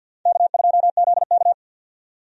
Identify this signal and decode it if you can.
Morse code, heard as a keyed tone, about 50 wpm, 700 Hz. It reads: K2ZK
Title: USA Call Signs - 4 Characters - 50wpm